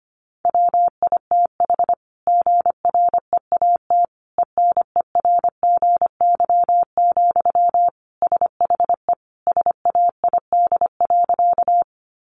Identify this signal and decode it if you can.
Morse code, heard as a keyed tone, about 25 wpm, 700 Hz. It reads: WITH GREAT ENERGY, SHE SAID.
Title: Sentences from Top 900 Words 25wpm